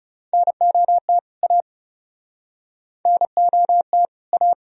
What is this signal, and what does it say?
Morse code, heard as a keyed tone, about 40 wpm, 700 Hz. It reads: NOT A NOT A